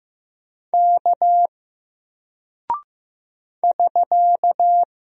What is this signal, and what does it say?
Morse code, heard as a keyed tone, about 15 wpm, 700 Hz. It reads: K <SK>
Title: Single Character (Letter:Number:Punctuation:Prosign) 15wpm